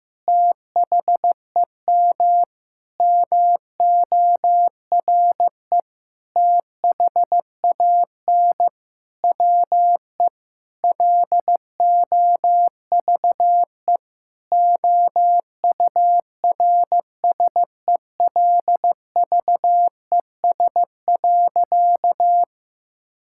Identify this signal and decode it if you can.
Morse code, heard as a keyed tone, about 15 wpm, 700 Hz. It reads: THEM MORE THAN WE LOVE OURSELVES.
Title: Sentences from Top 1600 Words 15wpm